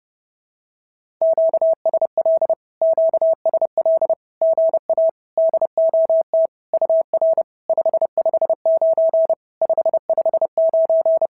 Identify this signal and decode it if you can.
Morse code, heard as a keyed tone, about 28 wpm, 650 Hz. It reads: QSL QSL GA DOT UR 559 559
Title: SOTA QSO Practice 30wpm